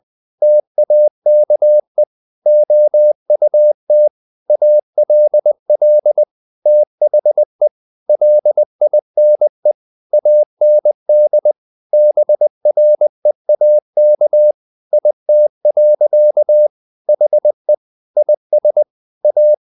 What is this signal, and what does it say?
Morse code, heard as a keyed tone, about 20 wpm, 600 Hz. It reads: TAKE OUT ALL THE LINE AND BREAK IT. HE IS A
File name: the old man and the sea / the old man and the sea part 1 20wpm